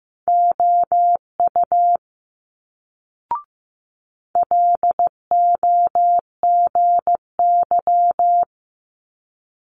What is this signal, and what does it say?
Morse code, heard as a keyed tone, about 15 wpm, 700 Hz. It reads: OU LOGY
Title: Multi-letter Phonemes Quick Fire 15wpm